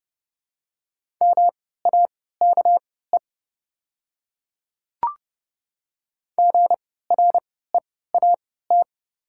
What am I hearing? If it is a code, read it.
Morse code, heard as a keyed tone, about 30 wpm, spaced as for 17 wpm, 700 Hz. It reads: MAKE GREAT
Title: 3x Character Spacing - Top 300 Words - 30wpm